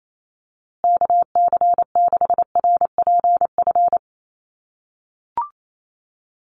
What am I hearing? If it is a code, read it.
Morse code, heard as a keyed tone, about 28 wpm, 700 Hz. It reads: KC6RPF
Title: USA Call Signs Repeated 3x 28wpm